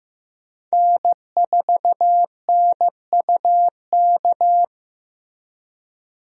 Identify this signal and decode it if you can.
Morse code, heard as a keyed tone, about 16 wpm, 700 Hz. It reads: N4NUK
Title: USA Call Signs - 5 Characters - Repeated 2x - 15wpm